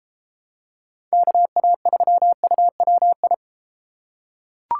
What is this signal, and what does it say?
Morse code, heard as a keyed tone, about 33 wpm, 700 Hz. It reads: KA3UWI